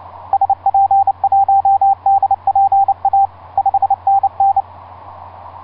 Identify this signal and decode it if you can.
Morse code, heard as a keyed tone, about 28 wpm, 790 Hz, with radio noise. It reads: SP1DPA 5NN
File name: s79gm s59z etc 10m cw 30mar2010 1126